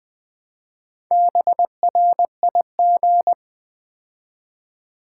Morse code, heard as a keyed tone, about 21 wpm, 700 Hz. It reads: BRIG
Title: Build A Word - 2 Letters At A Time - 6-Letter Words - 20wpm